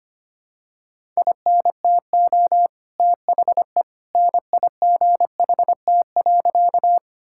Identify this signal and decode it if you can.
Morse code, heard as a keyed tone, about 25 wpm, 700 Hz. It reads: INTO THE NIGHT.
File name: practice / 25wpm / Sentences with 2-Word Phrases - 25wpm